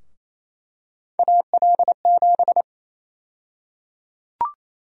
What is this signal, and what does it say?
Morse code, heard as a keyed tone, about 28 wpm, 700 Hz. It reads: AL7
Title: USA Call Sign Prefixes - 28wpm